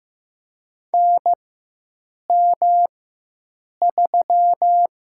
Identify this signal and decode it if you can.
Morse code, heard as a keyed tone, about 15 wpm, 700 Hz. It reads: N M 3
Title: USA Call Signs - 4 Characters - 4x Character Spacing - 15wpm